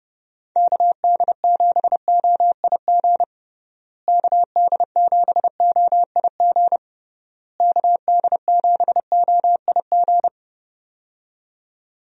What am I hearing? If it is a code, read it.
Morse code, heard as a keyed tone, about 31 wpm, 700 Hz. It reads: KD7OIG KD7OIG KD7OIG
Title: USA Call Signs - 6 Characters - Repeated 3x - 30wpm